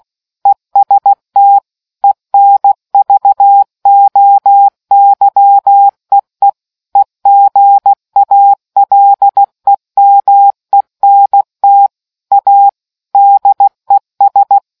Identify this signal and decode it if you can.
Morse code, heard as a keyed tone, about 16 wpm, 800 Hz. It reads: EST ENVOYEE EGALEMENT A DES